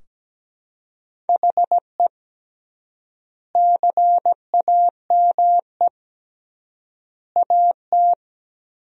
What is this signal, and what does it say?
Morse code, heard as a keyed tone, about 17 wpm, 700 Hz. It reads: HE CAME AT
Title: 3x Word-Spacing - Sentences from Top 300 Words - 17wpm